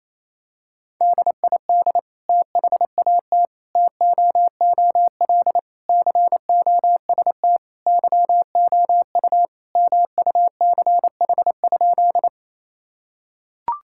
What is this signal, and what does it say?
Morse code, heard as a keyed tone, about 28 wpm, 700 Hz. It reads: DID THAT TOOL COST YOU MUCH?
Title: Sentences from Top 1000 Words - Review Part 6 - 28wpm